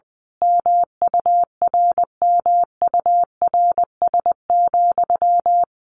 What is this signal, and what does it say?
Morse code, heard as a keyed tone, about 20 wpm, 700 Hz. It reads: MURMURS,
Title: A Sound Of Thunder 20wpm